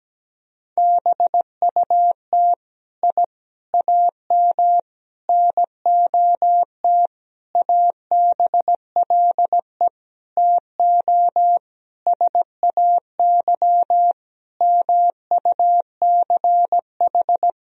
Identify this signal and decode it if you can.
Morse code, heard as a keyed tone, about 17 wpm, 700 Hz. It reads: BUT I AM NOT ABLE TO SAY MUCH
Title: Sentences from Top 1000 Words - Review Part 2 - 17wpm